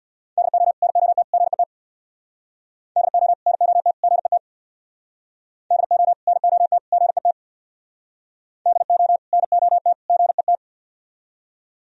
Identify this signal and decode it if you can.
Morse code, heard as a keyed tone, about 62 wpm, 700 Hz. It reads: DO NOT GET DO NOT GET DO NOT GET DO NOT GET
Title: Speed-Racing - Sets of 3 Words 50wpm